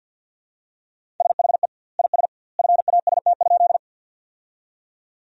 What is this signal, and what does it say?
Morse code, heard as a keyed tone, about 48 wpm, 700 Hz. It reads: SHE IS FAST?
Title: 2x Word-Spacing - Sentences from Top 400 Words - 50wpm